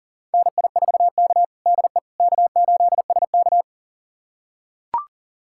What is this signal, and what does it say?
Morse code, heard as a keyed tone, about 40 wpm, 690 Hz. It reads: NI4K DE K8SK
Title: USA Callsign DE Callsign - 4 Characters - 40wpm